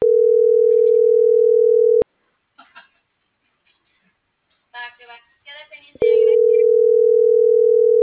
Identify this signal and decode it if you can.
Ringback tone (North America), 440 + 480 Hz.